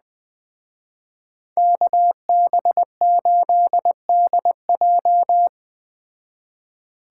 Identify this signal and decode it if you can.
Morse code, heard as a keyed tone, about 20 wpm, 700 Hz. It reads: KB8DJ